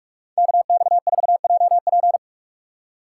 Morse code, heard as a keyed tone, about 41 wpm, 700 Hz. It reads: KX4JP